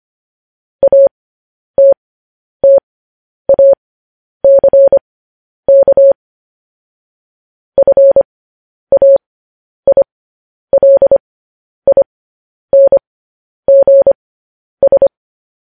Morse code, heard as a keyed tone, about 25 wpm, spaced as for 10 wpm, 550 Hz. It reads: ATTACK FAILINGS
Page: news-headlines-in-morse-code-at-10-wpm